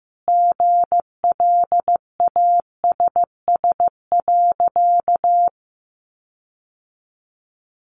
Morse code, heard as a keyed tone, about 15 wpm, 700 Hz. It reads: GLASS.
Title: Sentences from Top 1300 Words 15wpm